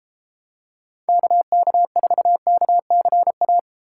Morse code, heard as a keyed tone, about 31 wpm, 700 Hz. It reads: KK4KCA